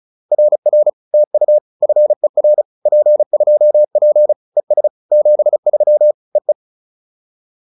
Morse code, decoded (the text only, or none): RR TU FER P2P ES 73 EE